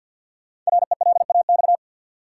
Morse code, heard as a keyed tone, about 52 wpm, 700 Hz. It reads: RELAX
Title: Sentences from Top 1400 Words - New Word Drill - 50wpm